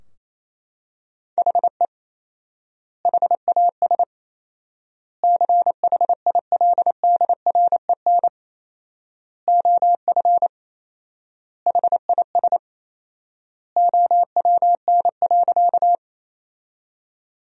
Morse code, heard as a keyed tone, about 28 wpm, 700 Hz. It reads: HE HAS CHILDREN OF HIS OWN.
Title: 4x Word-Spacing - Sentences from Top 300 Words - 28wpm